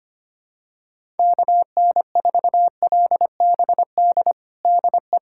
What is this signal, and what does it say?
Morse code, heard as a keyed tone, about 25 wpm, 700 Hz. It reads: KN4LBD DE
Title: USA Callsign DE Callsign - 6 Characters - 25wpm